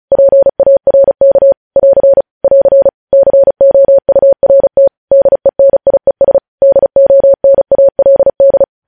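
Morse code, heard as a keyed tone, about 36 wpm, 550 Hz. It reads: PARK <AR> <AR> COURT DENIES DONALD
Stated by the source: News headlines sent in Morse code at 35 WPM.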